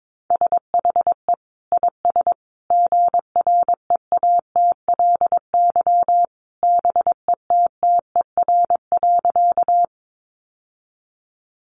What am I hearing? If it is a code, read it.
Morse code, heard as a keyed tone, about 22 wpm, 700 Hz. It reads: SHE IS GREATLY BETTER.
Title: Sentences from Top 1400 Words 22wpm